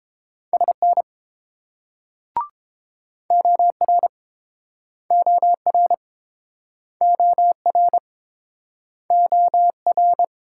Morse code, heard as a keyed tone, about 33 wpm, 700 Hz. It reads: SN OR OR OR OR